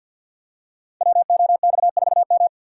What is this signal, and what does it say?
Morse code, heard as a keyed tone, about 52 wpm, 700 Hz. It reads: WOX4M
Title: Code Groups - 5 Letter:Numbers - 50wpm